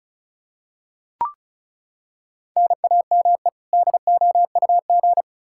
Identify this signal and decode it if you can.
Morse code, heard as a keyed tone, about 36 wpm, 700 Hz. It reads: NAME DOUG
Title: Practice QSO - Exchange 1 - Beginning - 35wpm